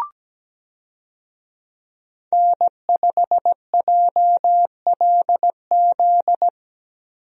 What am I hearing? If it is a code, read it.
Morse code, heard as a keyed tone, about 17 wpm, 700 Hz. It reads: N5JLZ
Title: USA Call Signs - 5 Characters - Repeated 2x - 17wpm